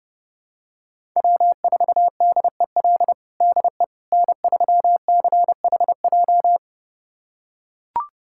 Morse code, heard as a keyed tone, about 30 wpm, 700 Hz. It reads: W4DEL DE N3CHJ
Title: USA Callsign DE Callsign - 5 Characters - 30wpm